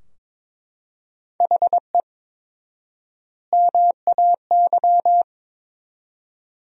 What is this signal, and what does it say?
Morse code, heard as a keyed tone, about 22 wpm, 700 Hz. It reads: HE MAY
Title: 4x Word-Spacing - Sentences from Top 300 Words - 22wpm